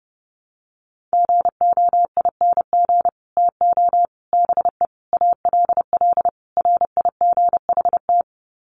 Morse code, heard as a keyed tone, about 30 wpm, 700 Hz. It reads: GOING TO BE ALL RIGHT
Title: Sets of 5 Words Encore2 30wpm